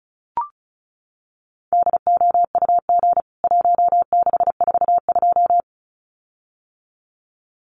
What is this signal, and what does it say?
Morse code, heard as a keyed tone, about 34 wpm, 700 Hz. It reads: DOUG 1642
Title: CWT Exchange 35wpm